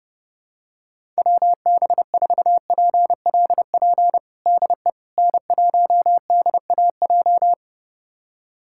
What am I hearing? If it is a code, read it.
Morse code, heard as a keyed tone, about 30 wpm, 700 Hz. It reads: WB4PLP DE N1DAJ
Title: USA Callsign DE Callsign 30wpm